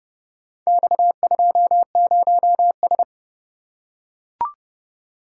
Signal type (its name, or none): Morse code